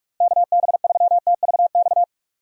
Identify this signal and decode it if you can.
Morse code, heard as a keyed tone, about 42 wpm, 700 Hz. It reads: KB3TVX
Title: USA Call Signs - 6 Characters - 45wpm